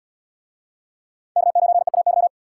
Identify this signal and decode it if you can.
Morse code, heard as a keyed tone, about 68 wpm, 700 Hz. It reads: K9EAY